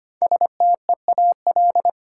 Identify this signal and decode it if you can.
Morse code, heard as a keyed tone, about 25 wpm, 700 Hz. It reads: STEAL
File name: 5 Letter Words - First 3 Letters Given - 25wpm